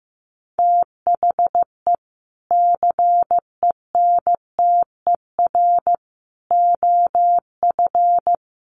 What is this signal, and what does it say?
Morse code, heard as a keyed tone, about 15 wpm, 700 Hz. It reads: THE CENTER OF
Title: Sets of 3 Words 15wpm